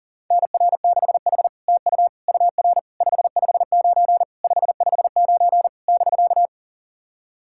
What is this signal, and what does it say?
Morse code, heard as a keyed tone, about 40 wpm, 700 Hz. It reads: NR6H TU UR 559 559 <BK>
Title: SOTA QSO Practice 40wpm